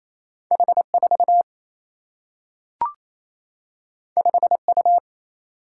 Morse code, heard as a keyed tone, about 28 wpm, 700 Hz. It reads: H 4 5 U
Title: Single-Letter Single-Number 28wpm